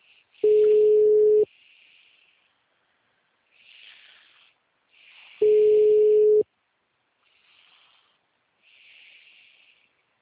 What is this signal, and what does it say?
Ringback tone (Europe, 425 Hz).